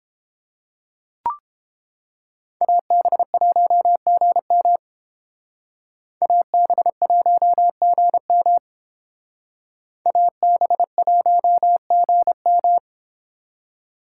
Morse code, heard as a keyed tone, about 30 wpm, 700 Hz. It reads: AB1GM AB1GM AB1GM